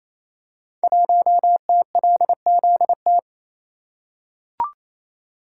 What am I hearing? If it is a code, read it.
Morse code, heard as a keyed tone, about 28 wpm, 700 Hz. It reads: 1TLZT
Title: Code Groups - 5 Letter:Numbers - 28wpm